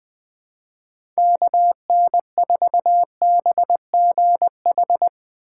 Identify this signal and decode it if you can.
Morse code, heard as a keyed tone, about 20 wpm, 700 Hz. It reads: KN4BGH
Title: USA Call Signs - 6 Characters - 4x Character Spacing - 20wpm